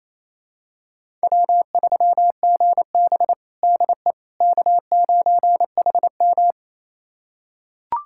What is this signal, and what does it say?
Morse code, heard as a keyed tone, about 28 wpm, 700 Hz. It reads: W3GB DE K9HM
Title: USA Callsign DE Callsign - 4 Characters - 28wpm